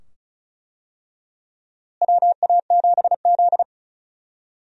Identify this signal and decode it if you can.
Morse code, heard as a keyed tone, about 36 wpm, 700 Hz. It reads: WA7Z
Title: USA Call Signs - 4 Characters - Repeated 2x - 35wpm